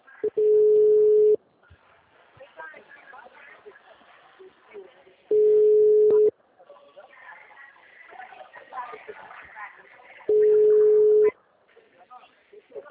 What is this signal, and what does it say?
Ringback tone (Europe, 425 Hz), with speech.